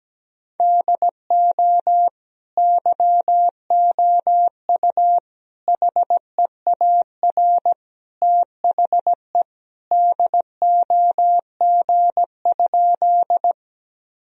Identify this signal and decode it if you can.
Morse code, heard as a keyed tone, about 17 wpm, 700 Hz. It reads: DO YOU HEAR THE DOG?
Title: Sentences from Top 1000 Words - Review Part 2 - 17wpm